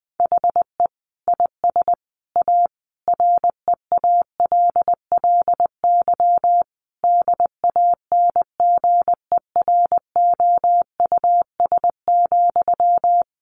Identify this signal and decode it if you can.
Morse code, heard as a keyed tone, about 20 wpm, 700 Hz. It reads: HE IS A REALLY DANGEROUS,
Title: Sentences from Top 1200 Words 20wpm